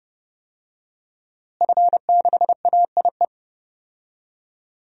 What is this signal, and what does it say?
Morse code, heard as a keyed tone, about 32 wpm, 700 Hz. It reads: F6AIE
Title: DXCC-100 Call Signs - 5 Characters - Repeated 3x - 30wpm